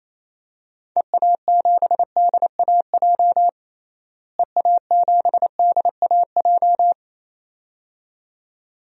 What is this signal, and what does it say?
Morse code, heard as a keyed tone, about 28 wpm, 700 Hz. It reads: EA7DAJ EA7DAJ
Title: DXCC-100 Call Signs - 6 Characters - Repeated 2x - 28wpm